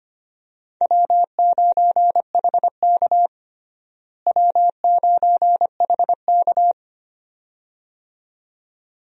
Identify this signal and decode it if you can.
Morse code, heard as a keyed tone, about 25 wpm, 700 Hz. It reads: W9HK W9HK
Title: USA Call Signs - 4 Characters - Repeated 2x - 25wpm